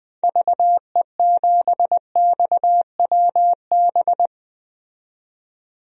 Morse code, heard as a keyed tone, about 20 wpm, 700 Hz. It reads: VE7XWB
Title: DXCC-100 Call Signs - 6 Characters - 20wpm